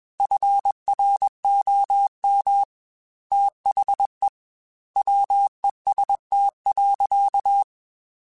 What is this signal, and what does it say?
Morse code, heard as a keyed tone, about 21 wpm, 800 Hz. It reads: FROM THE WEST.
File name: Weather Sets / Weather Sets-mp3s / Weather 5 1 @20